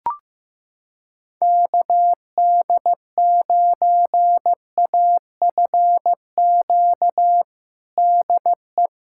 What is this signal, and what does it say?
Morse code, heard as a keyed tone, about 15 wpm, 700 Hz. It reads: KD9AFQ DE
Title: USA Callsign DE Callsign - 6 Characters - 15wpm